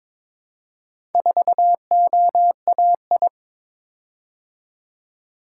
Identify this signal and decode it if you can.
Morse code, heard as a keyed tone, about 22 wpm, 700 Hz. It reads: 4OAI
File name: practice / 22wpm / Four Letter-Number Combinations - 22wpm